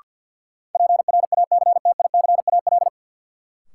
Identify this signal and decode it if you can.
Morse code, heard as a keyed tone, about 52 wpm, 700 Hz. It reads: PRACTICAL
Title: Sentences from Top 1400 Words - New Word Drill - 50wpm